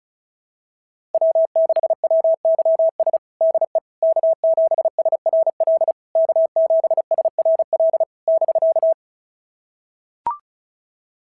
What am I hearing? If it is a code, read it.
Morse code, heard as a keyed tone, about 36 wpm, 660 Hz. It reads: W6WYS DE K7SRL K7SRL <BK>
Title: SOTA QSO Practice 35wpm